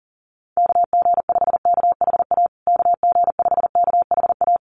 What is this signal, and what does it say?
Morse code, heard as a keyed tone, about 40 wpm, 700 Hz. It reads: KG5KHA KG5KHA
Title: USA Call Signs Repeated 3x 40wpm